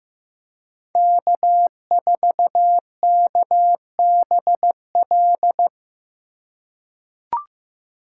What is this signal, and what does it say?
Morse code, heard as a keyed tone, about 15 wpm, 700 Hz. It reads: K4KBL
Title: K1USN SST QSO Practice - 15wpm